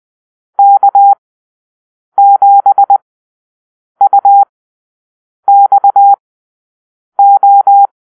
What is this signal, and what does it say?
Morse code, heard as a keyed tone, about 20 wpm, spaced as for 7 wpm, 800 Hz. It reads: K7UXO
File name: arrl / Call-Sign Drill-9